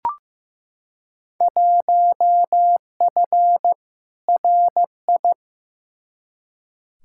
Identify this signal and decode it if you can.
Morse code, heard as a keyed tone, about 15 wpm, 700 Hz. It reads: 1F RI
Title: Field Day Exchanges Encore - 15wpm